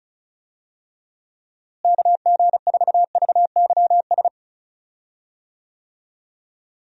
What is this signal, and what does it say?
Morse code, heard as a keyed tone, about 36 wpm, 700 Hz. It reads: KG4VYS
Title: USA Call Signs - 6 Characters - Repeated 2x - 35wpm